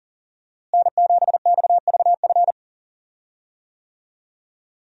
Morse code, heard as a keyed tone, about 40 wpm, 700 Hz. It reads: N7XVF
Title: USA Call Signs - 5 Characters - Repeated 2x - 40wpm